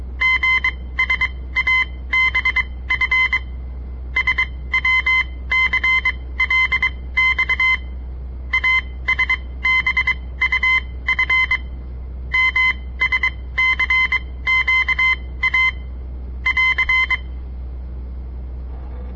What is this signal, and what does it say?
Morse code, heard as a keyed tone, about 22 wpm, spaced as for 17 wpm, 2050 Hz. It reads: GSABF SWCLX ASBUF MSCQA <AR>
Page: On the air Morse code practice from the Southern Montana Amatuer Radio Association; all code, no read back